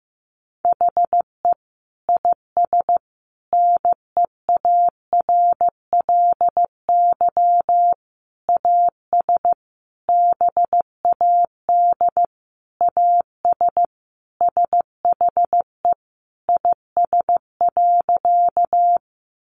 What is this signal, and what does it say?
Morse code, heard as a keyed tone, about 15 wpm, 700 Hz. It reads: HE IS NEARLY AS BAD AS SHE IS.
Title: Sentences from Top 1300 Words 15wpm